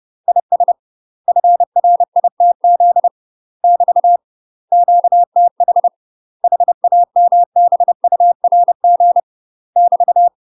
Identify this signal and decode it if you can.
Morse code, heard as a keyed tone, about 30 wpm, 700 Hz. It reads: IS FRITZ <BT> QTH HAMBURG <BT>
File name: UK MASTER QSOs TRAINING / 30WPM / UK MASTER QSOs 0030